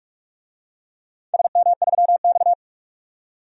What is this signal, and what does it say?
Morse code, heard as a keyed tone, about 46 wpm, 700 Hz. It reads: SM3X
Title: DXCC-100 Call Signs - 4 Characters - 45wpm